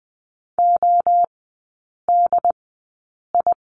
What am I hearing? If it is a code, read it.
Morse code, heard as a keyed tone, about 20 wpm, spaced as for 8 wpm, 690 Hz. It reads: ODI
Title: 087 Lesson 15e - Number 5 - Three Characters 20wpm